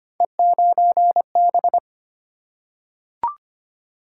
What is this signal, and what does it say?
Morse code, heard as a keyed tone, about 25 wpm, 700 Hz. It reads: E9B
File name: Three Letter-Number Combinations - 25wpm